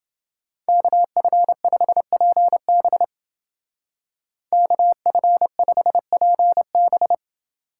Morse code, heard as a keyed tone, about 29 wpm, 700 Hz. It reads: KF5PB KF5PB